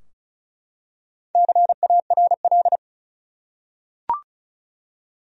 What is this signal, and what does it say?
Morse code, heard as a keyed tone, about 34 wpm, 700 Hz. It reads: CARL
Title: USA Common Names - 35wpm